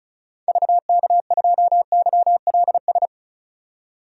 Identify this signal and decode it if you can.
Morse code, heard as a keyed tone, about 36 wpm, 700 Hz. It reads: VK2YLS
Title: DXCC-100 Call Signs - 6 Characters - 35wpm